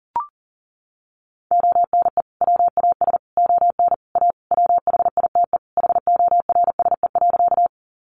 Morse code, heard as a keyed tone, about 40 wpm, 700 Hz. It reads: ONE WAS ON A WHITE HORSE.
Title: Sentences from Top 300 Words 40wpm